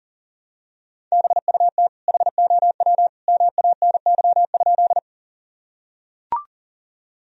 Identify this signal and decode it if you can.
Morse code, heard as a keyed tone, about 40 wpm, 700 Hz. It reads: BUT HOW MANY?
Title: Sentences from Top 1000 Words - Review Part 1 - 40wpm